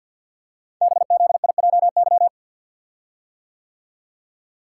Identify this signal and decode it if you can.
Morse code, heard as a keyed tone, about 52 wpm, 700 Hz. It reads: BZIJY
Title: Code Groups - 5 Letters - 50wpm